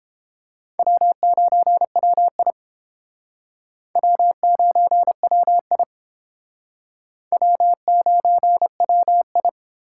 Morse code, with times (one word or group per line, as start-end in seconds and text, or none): 0.8-2.5 W9WI
4.0-5.8 W9WI
7.3-9.5 W9WI